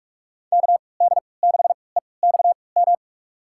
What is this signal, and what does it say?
Morse code, heard as a keyed tone, about 46 wpm, spaced as for 26 wpm, 700 Hz. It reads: KD6EXM
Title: USA Call Signs - 6 Characters - 3x Character Spacing - 45wpm